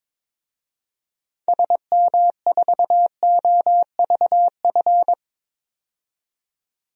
Morse code, heard as a keyed tone, about 22 wpm, 700 Hz. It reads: SM4OVF
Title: DXCC-100 Call Signs - 6 Characters - 22wpm